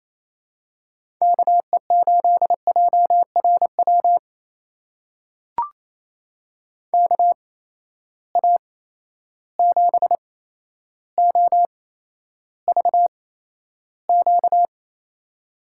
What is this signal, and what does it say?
Morse code, heard as a keyed tone, about 28 wpm, spaced as for 12 wpm, 700 Hz. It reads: KE8JRW KA7OVQ
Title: USA Call Signs - 6 Characters - 8x Character Spacing - 28wpm